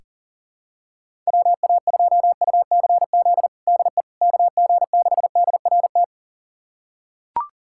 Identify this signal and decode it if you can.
Morse code, heard as a keyed tone, about 40 wpm, 700 Hz. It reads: WA2UCZ DE KG6DRT
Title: USA Callsign DE Callsign - 6 Characters - 40wpm